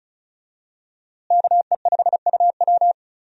Morse code, heard as a keyed tone, about 36 wpm, 700 Hz. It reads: KE5UW
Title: USA Call Signs - 5 Characters - Repeated 2x - 35wpm